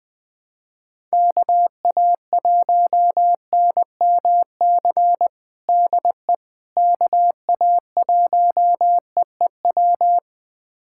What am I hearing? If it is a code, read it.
Morse code, heard as a keyed tone, about 21 wpm, 700 Hz. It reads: KA1NMC DE KA1EEW
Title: USA Callsign DE Callsign 20wpm